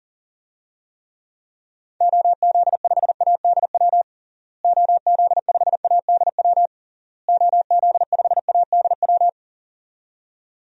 Morse code, heard as a keyed tone, about 40 wpm, 700 Hz. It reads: OZ5ADW OZ5ADW OZ5ADW
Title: DXCC-100 Call Signs - 6 Characters - Repeated 3x - 40wpm